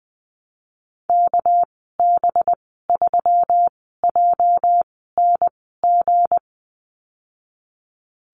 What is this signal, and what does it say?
Morse code, heard as a keyed tone, about 20 wpm, spaced as for 15 wpm, 700 Hz. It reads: KB3JNG
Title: 207 Lesson 33g - Letter j - Call Signs 20wpm